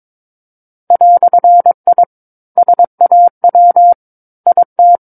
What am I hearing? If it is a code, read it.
Morse code, heard as a keyed tone, about 22 wpm, 700 Hz. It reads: "I SAW IT
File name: Book One / 13-Chapter 13 - 22 WPM